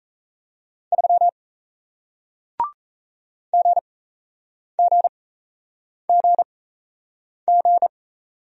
Morse code, heard as a keyed tone, about 38 wpm, spaced as for 8 wpm, 700 Hz. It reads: 3 GGGG